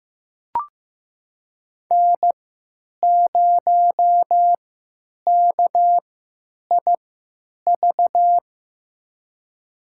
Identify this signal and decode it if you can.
Morse code, heard as a keyed tone, about 15 wpm, spaced as for 9 wpm, 700 Hz. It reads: N0KIV
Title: USA Call Signs - 5 Characters - 3x Character Spacing - 15wpm